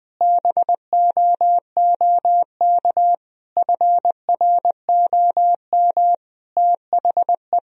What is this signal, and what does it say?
Morse code, heard as a keyed tone, about 20 wpm, 700 Hz. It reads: BOOK FROM THE
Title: Sentences from Top 1000 Words - Review Part 4 - 20wpm